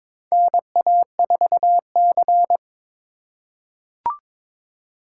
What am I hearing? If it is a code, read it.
Morse code, heard as a keyed tone, about 22 wpm, 700 Hz. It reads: NA4C
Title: K1USN SST QSO Practice - 22wpm